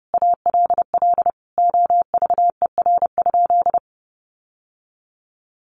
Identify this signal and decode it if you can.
Morse code, heard as a keyed tone, about 30 wpm, 700 Hz. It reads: ALL OVER?
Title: Sentences from Top 1200 Words 30wpm